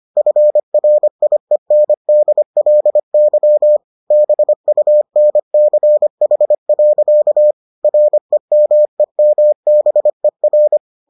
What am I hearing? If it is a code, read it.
Morse code, heard as a keyed tone, about 25 wpm, 600 Hz. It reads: FRIENDLY BUNCH. REMEMBER